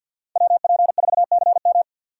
Morse code, heard as a keyed tone, about 51 wpm, 700 Hz. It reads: WP4CM
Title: USA Call Signs - 5 Characters - 4x Character Spacing - 50wpm